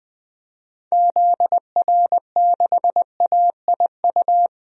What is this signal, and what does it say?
Morse code, heard as a keyed tone, about 20 wpm, 700 Hz. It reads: ZR6AIU
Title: DXCC-100 Call Signs - 6 Characters - 20wpm